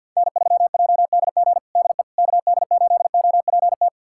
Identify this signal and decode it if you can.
Morse code, heard as a keyed tone, about 50 wpm, 700 Hz. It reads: N3JDG DE KD8OPT
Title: USA Callsign DE Callsign 50wpm